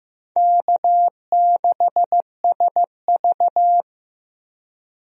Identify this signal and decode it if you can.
Morse code, heard as a keyed tone, about 15 wpm, 700 Hz. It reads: K6SV
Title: USA Call Signs - 4 Characters - Repeated 2x - 15wpm